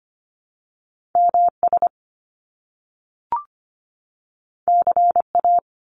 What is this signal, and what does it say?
Morse code, heard as a keyed tone, about 25 wpm, 700 Hz. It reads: MS CA
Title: US States 25wpm